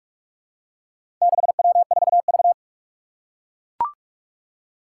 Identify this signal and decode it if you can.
Morse code, heard as a keyed tone, about 44 wpm, 700 Hz. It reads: 6W4V
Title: Four Challenging Letter-Number Combinations 45wpm